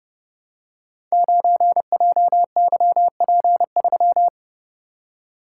Morse code, heard as a keyed tone, about 30 wpm, 700 Hz. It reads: 9JYP3
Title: Code Groups - 5 Letter:Numbers - 30wpm